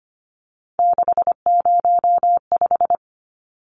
Morse code, heard as a keyed tone, about 25 wpm, 700 Hz. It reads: 605
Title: Numbers - 3-Digit 25wpm